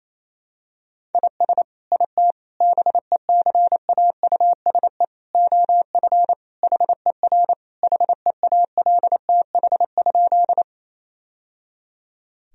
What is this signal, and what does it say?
Morse code, heard as a keyed tone, about 27 wpm, 700 Hz. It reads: IS IT BECAUSE OF HER HEALTH?
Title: Sentences from Top 1000 Words - Review Part 3 - 28wpm